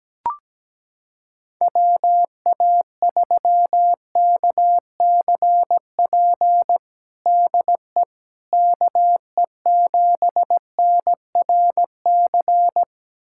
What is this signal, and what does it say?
Morse code, heard as a keyed tone, about 17 wpm, 700 Hz. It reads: WA3KCP DE KE7NRC
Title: USA Callsign DE Callsign - 6 Characters - 17wpm